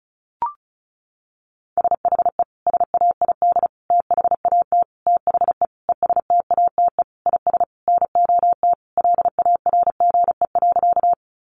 Morse code, heard as a keyed tone, about 34 wpm, 700 Hz. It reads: SHE SAID THAT THE ESTATE IS NOT LARGE.
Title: Sentences from Top 1200 Words 35wpm